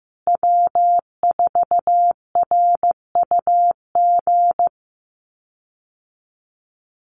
Morse code, heard as a keyed tone, about 15 wpm, 700 Hz. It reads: W4RUG
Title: USA Call Signs Repeated 2x 15wpm